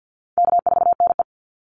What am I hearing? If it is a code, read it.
Morse code, heard as a keyed tone, about 48 wpm, 700 Hz. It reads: K4NE
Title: Call Signs Repeated 3x - 50wpm